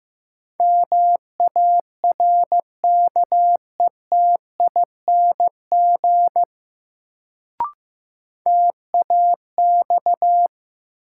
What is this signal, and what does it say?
Morse code, heard as a keyed tone, about 15 wpm, 700 Hz. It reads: MARKETING TAX
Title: Sentences from Top 1000 Words - Review Part 5 - New Word Drill - 15wpm